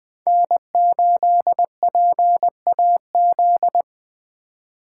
Morse code, heard as a keyed tone, about 20 wpm, 700 Hz. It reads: N8PAZ
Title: Speed Racing - USA 1x3 Call Signs 17wpm